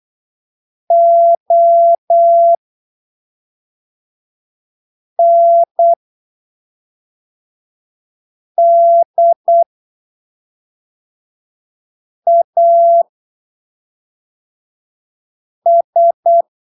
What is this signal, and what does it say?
Morse code, heard as a keyed tone, about 8 wpm, spaced as for 3 wpm, 680 Hz. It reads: ONDAS
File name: arquivos / texto / texto 13 08ppm